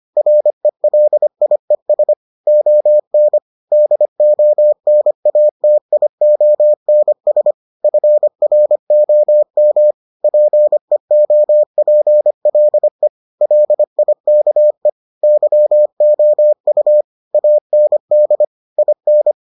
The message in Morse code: RELIES ON DONATIONS FROM PEOPLE LIKE YOU AND IN

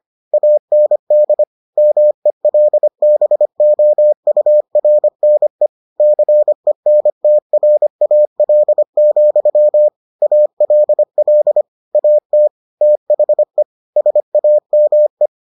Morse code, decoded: AND MELBOURNE CENTRAL, ALL AT THE SAME

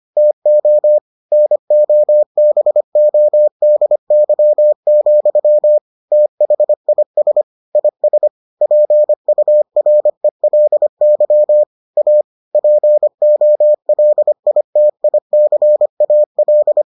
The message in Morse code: TO NOBODY, THIS IS PURELY A POLITICAL

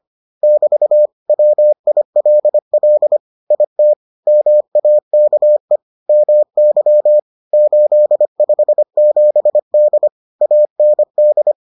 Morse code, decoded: <BT> WILL IT MAKE MY 857D AND

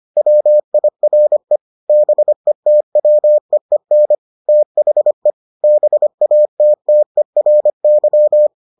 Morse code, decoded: WIRE BETWEEN THE BATTERY

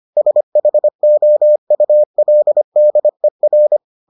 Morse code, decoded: SHOULDER

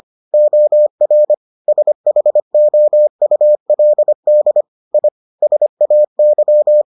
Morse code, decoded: OR SHOULD I SAY

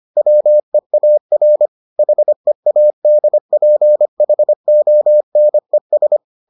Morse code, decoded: WEAR HEADPHONES

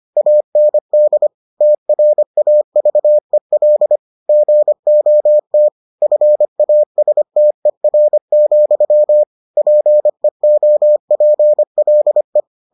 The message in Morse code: AND TRAVEL GOT FASTER, PEOPLE

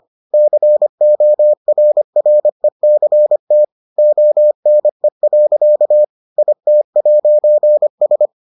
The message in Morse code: CORRECT ONE. IT'S

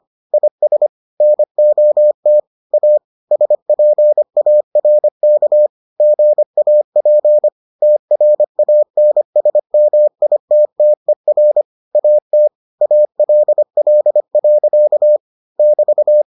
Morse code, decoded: IS NOT A SPARK GAP TRANSMITTER AT ALL. <BT>

